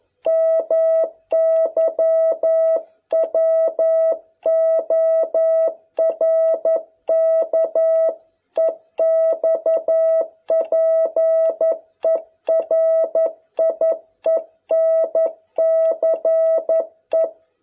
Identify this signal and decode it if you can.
Morse code, heard as a keyed tone, about 11 wpm, 660 Hz. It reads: MY WORK EXPERIENCE